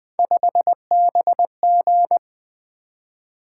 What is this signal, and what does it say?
Morse code, heard as a keyed tone, about 20 wpm, 700 Hz. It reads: 5BG
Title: Three Letter-Number Combinations - 20wpm